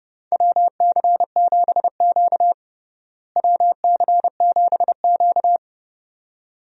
Morse code, heard as a keyed tone, about 32 wpm, 700 Hz. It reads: WC7Q WC7Q
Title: USA Call Signs - 4 Characters - Repeated 2x - 30wpm